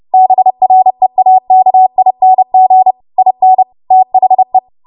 Morse code, heard as a keyed tone, about 30 wpm, 750 Hz. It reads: BREAKING IN THE
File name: Morse / 30WPM